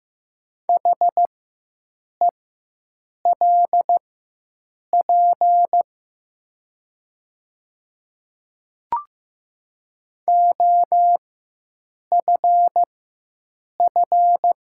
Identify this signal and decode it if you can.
Morse code, heard as a keyed tone, about 15 wpm, spaced as for 7 wpm, 700 Hz. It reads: HELP OFF
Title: 4x Character Spacing - Top 200 Words - 15wpm